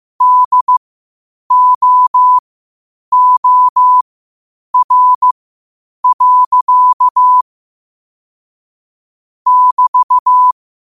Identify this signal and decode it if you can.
Morse code, heard as a keyed tone, about 15 wpm, spaced as for 8 wpm, 1000 Hz. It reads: DOOR. <BT>
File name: qotdfiles / 8wpm